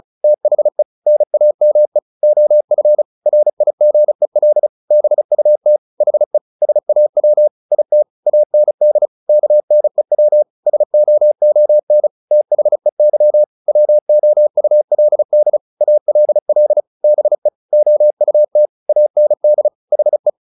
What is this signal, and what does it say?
Morse code, heard as a keyed tone, about 34 wpm, 600 Hz. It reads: THE NAME OF RIGEL BUT HE SAW IT AND KNEW SOON THEY WOULD ALL BE OUT AND HE
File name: the old man and the sea 2 35wpm